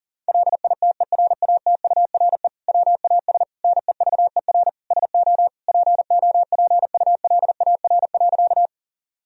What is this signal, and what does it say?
Morse code, heard as a keyed tone, about 40 wpm, 700 Hz. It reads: LITERATURE WAS NEVER SO POPULAR.
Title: Sentences from Top 1000 Words - Review Part 3 - 40wpm